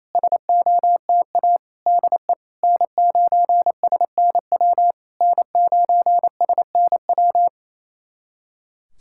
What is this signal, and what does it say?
Morse code, heard as a keyed tone, about 28 wpm, 700 Hz. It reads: SOTA DE N9SNW N9SNW
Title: SOTA QSO Practice 28wpm